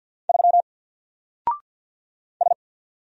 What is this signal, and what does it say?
Morse code, heard as a keyed tone, about 50 wpm, 700 Hz. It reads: 3 S
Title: Single Character (Letter:Number:Punctuation:Prosign) 50wpm